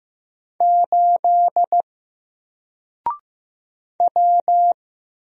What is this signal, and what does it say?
Morse code, heard as a keyed tone, about 15 wpm, 700 Hz. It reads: 8 W
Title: Single Character (Letter:Number:Punctuation:Prosign) 15wpm